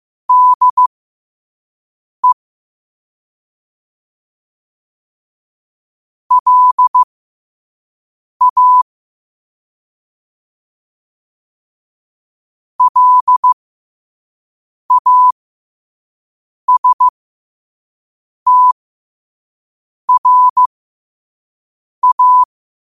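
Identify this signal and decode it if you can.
Morse code, heard as a keyed tone, about 15 wpm, spaced as for 5 wpm, 1000 Hz. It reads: DE LA LASTRA
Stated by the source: Morse code quote of the day sent at 5 wpm.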